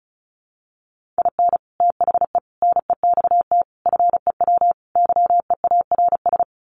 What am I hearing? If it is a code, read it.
Morse code, heard as a keyed tone, about 34 wpm, 700 Hz. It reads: IN THE NEXT FEW YEARS
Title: Sets of 5 Words Encore 35wpm